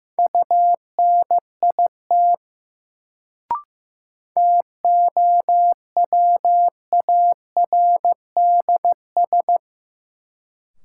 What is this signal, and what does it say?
Morse code, heard as a keyed tone, about 15 wpm, 700 Hz. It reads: UNIT TOWARDS
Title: Sentences from Top 1000 Words - Review Part 2 - New Word Drill - 15wpm